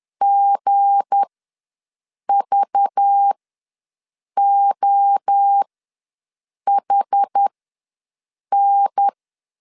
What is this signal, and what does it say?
Morse code, heard as a keyed tone, about 10 wpm, spaced as for 6 wpm, 800 Hz. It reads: GVOHN
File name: LEZIONI / 7 GF9Z / 7 G ws 10wpm 3843